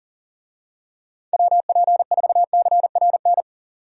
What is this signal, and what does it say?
Morse code, heard as a keyed tone, about 40 wpm, 700 Hz. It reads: WP4CRN
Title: USA Call Signs - 6 Characters - 40wpm